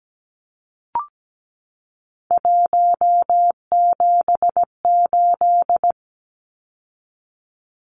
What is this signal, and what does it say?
Morse code, heard as a keyed tone, about 17 wpm, 700 Hz. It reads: 178
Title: Numbers - 3-Digit 17wpm